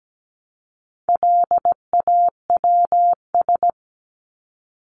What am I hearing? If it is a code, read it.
Morse code, heard as a keyed tone, about 17 wpm, 700 Hz. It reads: LAWS
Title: ES:S Suffix - Top 100 Words with Base Words 17wpm